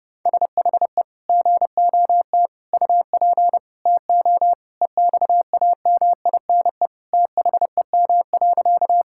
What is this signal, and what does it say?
Morse code, heard as a keyed tone, about 30 wpm, 700 Hz. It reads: SHE GOT UP TO EXAMINE THEM.
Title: Sentences from Top 1600 Words 30wpm